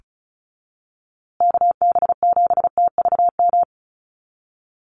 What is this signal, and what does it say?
Morse code, heard as a keyed tone, about 34 wpm, 700 Hz. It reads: KB7TVM
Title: USA Call Signs Repeated 3x 35wpm